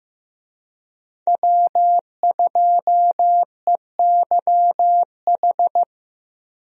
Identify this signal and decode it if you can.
Morse code, heard as a keyed tone, about 15 wpm, 700 Hz. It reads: W2EYH
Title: USA Call Signs - 5 Characters - Repeated 2x - 15wpm